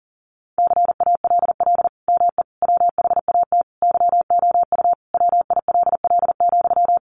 Morse code, heard as a keyed tone, about 40 wpm, 700 Hz. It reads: CALL ME WHAT YOU WILL,
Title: Sentences from Top 200 Words 40wpm